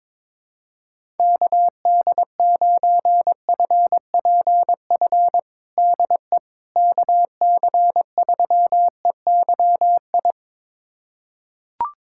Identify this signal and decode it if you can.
Morse code, heard as a keyed tone, about 22 wpm, 700 Hz. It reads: KD9FPF DE KC3EYI
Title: USA Callsign DE Callsign - 6 Characters - 22wpm